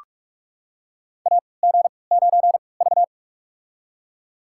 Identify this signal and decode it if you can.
Morse code, heard as a keyed tone, about 44 wpm, spaced as for 25 wpm, 700 Hz. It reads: AG9V
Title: USA Call Signs - 4 Characters - 3x Character Spacing - 45wpm